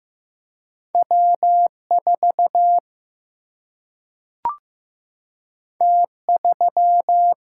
Morse code, heard as a keyed tone, about 15 wpm, 700 Hz. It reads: W 4 T 3
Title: Single-Letter Single-Number 15wpm